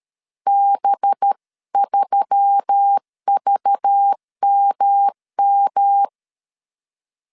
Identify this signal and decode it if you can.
Morse code, heard as a keyed tone, about 13 wpm, 800 Hz. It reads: B3VMM
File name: CW / DET / 1 6 1500C S B 12wpm 3431